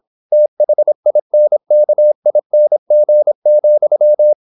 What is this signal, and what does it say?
Morse code, heard as a keyed tone, about 26 wpm, 600 Hz. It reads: THINKING,
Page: Never 26wpm | CW med Gnister